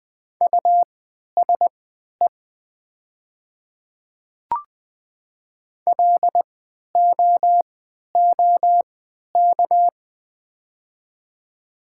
Morse code, heard as a keyed tone, about 20 wpm, spaced as for 11 wpm, 700 Hz. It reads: USE LOOK
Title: 3x Character Spacing - Top 200 Words - 20wpm